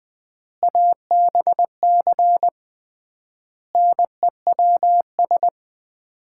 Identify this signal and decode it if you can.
Morse code, heard as a keyed tone, about 20 wpm, 700 Hz. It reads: ABC NEWS
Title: 3x Word-Spacing - Sets of 5 Words - 20wpm